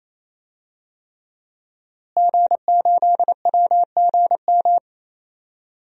Morse code, heard as a keyed tone, about 28 wpm, 700 Hz. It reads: G8WGM
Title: DXCC-100 Call Signs - 5 Characters - Repeated 3x - 28wpm